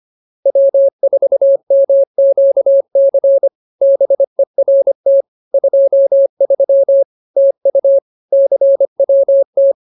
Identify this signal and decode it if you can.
Morse code, heard as a keyed tone, about 25 wpm, 550 Hz. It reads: W4MQC BERT 23 TU CWT
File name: CWT-209-25